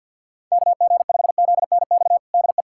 Morse code, heard as a keyed tone, about 51 wpm, 690 Hz. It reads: KG5ZNX DE
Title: USA Callsign DE Callsign 50wpm